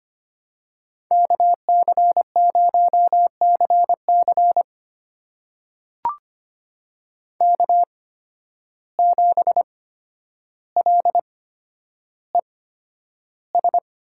Morse code, heard as a keyed tone, about 25 wpm, spaced as for 11 wpm, 700 Hz. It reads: KC0CC K7LES
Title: USA Call Signs - 5 Characters - 8x Character Spacing - 25wpm